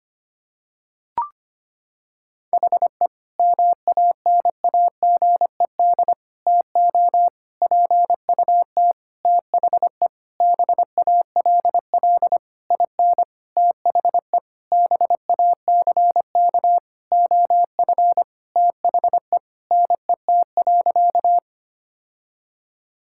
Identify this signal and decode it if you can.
Morse code, heard as a keyed tone, about 25 wpm, 700 Hz. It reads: HE MANAGED TO PUT THE BALL IN THE BACK OF THE NET.
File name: practice / 25wpm / Sentences with 9-Word Phrases - 25wpm